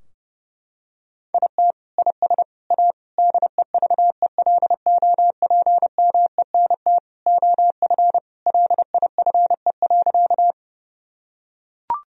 Morse code, heard as a keyed tone, about 28 wpm, 700 Hz. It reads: IT IS A DEVELOPMENT OF LIFE.
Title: Sentences from Top 1000 Words - Review Part 4 - 30wpm